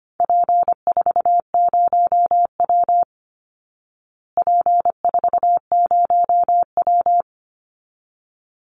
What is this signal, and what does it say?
Morse code, heard as a keyed tone, about 25 wpm, 700 Hz. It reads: P40W P40W
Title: Call Signs Repeated 3x - 25wpm